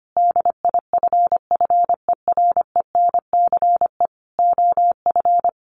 Morse code, heard as a keyed tone, about 25 wpm, 700 Hz. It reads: DIFFERENCE OF
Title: Sentences from Top 1400 Words 25wpm